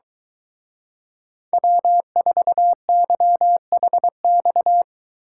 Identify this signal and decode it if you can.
Morse code, heard as a keyed tone, about 23 wpm, 700 Hz. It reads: W4YHX